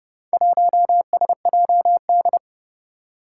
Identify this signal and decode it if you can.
Morse code, heard as a keyed tone, about 31 wpm, 700 Hz. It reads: 1SJD
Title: Four Challenging Letter-Number Combinations 30wpm